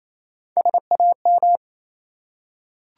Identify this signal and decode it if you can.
Morse code, heard as a keyed tone, about 27 wpm, 700 Hz. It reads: SAM
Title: USA Common Names - Max 4 Letters - 28wpm